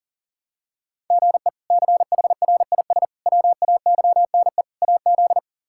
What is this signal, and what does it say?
Morse code, heard as a keyed tone, about 40 wpm, 700 Hz. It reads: GE CHRIS WAYNE AZ
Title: K1USN SST QSO Practice - 40wpm